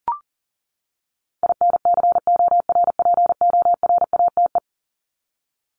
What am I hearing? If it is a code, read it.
Morse code, heard as a keyed tone, about 40 wpm, 700 Hz. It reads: INCORPORATE
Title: Top 1751-2000 Words 40wpm